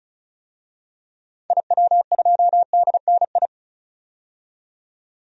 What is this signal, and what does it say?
Morse code, heard as a keyed tone, about 36 wpm, 700 Hz. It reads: IW2DNI
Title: DXCC-100 Call Signs - 6 Characters - 35wpm